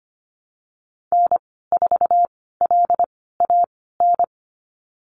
Morse code, heard as a keyed tone, about 25 wpm, spaced as for 16 wpm, 700 Hz. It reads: N4LAN
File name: Lesson 15g - Number 5 - Call Signs 25wpm